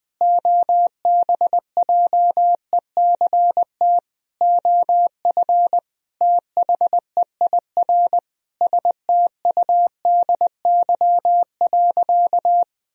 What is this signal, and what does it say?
Morse code, heard as a keyed tone, about 21 wpm, 700 Hz. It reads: OBJECT OF THEIR STUDY.
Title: Sentences from Top 1000 Words - Review Part 3 - 20wpm